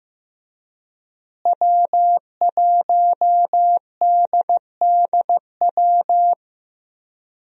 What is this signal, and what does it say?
Morse code, heard as a keyed tone, about 15 wpm, 700 Hz. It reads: W1DDW
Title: USA Call Signs - 5 Characters - Repeated 2x - 15wpm